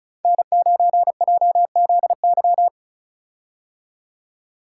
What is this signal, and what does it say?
Morse code, heard as a keyed tone, about 34 wpm, 700 Hz. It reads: N9JZY
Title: K1USN SST QSO Practice - 35wpm